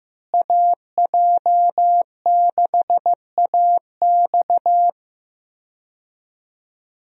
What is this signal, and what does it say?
Morse code, heard as a keyed tone, about 15 wpm, 700 Hz. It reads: AJ6AX
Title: USA Call Signs - 5 Characters - Repeated 3x - 15wpm